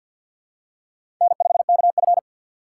Morse code, heard as a keyed tone, about 52 wpm, 700 Hz. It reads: N5KF
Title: USA Call Signs - 4x Character Spacing - 50wpm